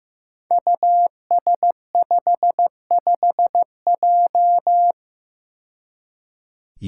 Morse code, heard as a keyed tone, about 15 wpm, 700 Hz. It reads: US55J
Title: Five Challenging Letter-Number Combinations 15wpm